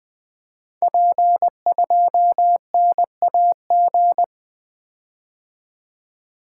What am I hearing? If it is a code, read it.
Morse code, heard as a keyed tone, about 20 wpm, 700 Hz. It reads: P2NAG
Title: Code Groups - 5 Letter:Numbers - 20wpm